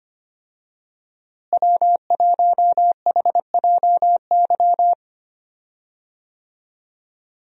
Morse code, heard as a keyed tone, about 25 wpm, 700 Hz. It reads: W1HJY
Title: USA Call Signs - 5 Characters - Repeated 2x - 25wpm